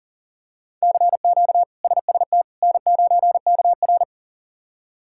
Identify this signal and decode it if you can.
Morse code, heard as a keyed tone, about 40 wpm, 700 Hz. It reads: CQ SST N9KR
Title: K1USN SST QSO Practice - 40wpm